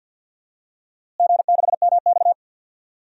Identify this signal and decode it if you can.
Morse code, heard as a keyed tone, about 47 wpm, 700 Hz. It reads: G6MX
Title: DXCC-100 Call Signs - 4 Characters - Repeated 3x - 50wpm